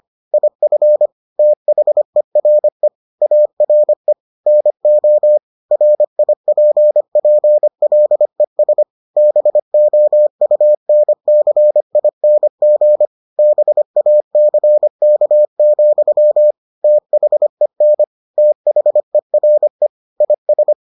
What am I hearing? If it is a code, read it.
Morse code, heard as a keyed tone, about 25 wpm, 600 Hz. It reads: IF THERE ARE NO RIPPLES BOUNCING BACK, THEN THERE IS